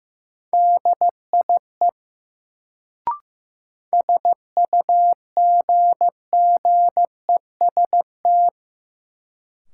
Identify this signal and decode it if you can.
Morse code, heard as a keyed tone, about 15 wpm, 700 Hz. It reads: DIE SUGGEST
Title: Sentences from Top 1300 Words - New Word Drill - 15wpm